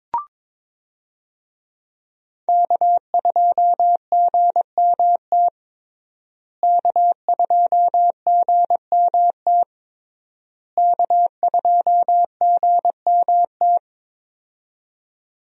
Morse code, heard as a keyed tone, about 22 wpm, 700 Hz. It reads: K2GMT K2GMT K2GMT
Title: USA Call Signs - 5 Characters - Repeated 3x - 22wpm